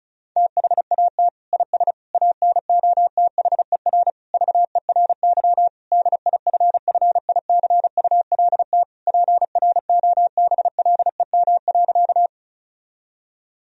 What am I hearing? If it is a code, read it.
Morse code, heard as a keyed tone, about 34 wpm, 700 Hz. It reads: THAT IS ANOTHER VERY DIFFICULT PROBLEM.
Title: Sentences from Top 1000 Words - Review Part 2 - 35wpm